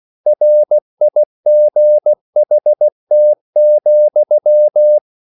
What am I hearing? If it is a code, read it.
Morse code, heard as a keyed tone, about 16 wpm, 600 Hz. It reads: RIGHT,
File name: CW / Practice / EnglishQuotations0416